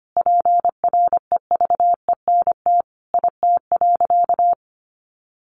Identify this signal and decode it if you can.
Morse code, heard as a keyed tone, about 25 wpm, 700 Hz. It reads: PREVENT IT.
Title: Sentences from Top 1100 Words 25wpm